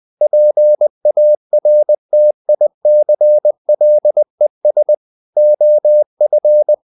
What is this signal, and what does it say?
Morse code, heard as a keyed tone, about 20 wpm, 600 Hz. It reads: PARTICLES OF
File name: the old man and the sea part 2 20wpm